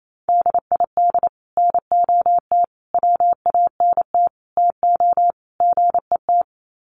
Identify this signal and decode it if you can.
Morse code, heard as a keyed tone, about 28 wpm, 700 Hz. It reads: DID NOT WANT TO GET
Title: Sets of 5 Words Encore2 28wpm